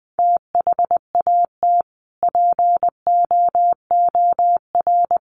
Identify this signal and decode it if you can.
Morse code, heard as a keyed tone, about 20 wpm, 700 Hz. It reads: THAT POOR
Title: Sentences from Top 1100 Words 20wpm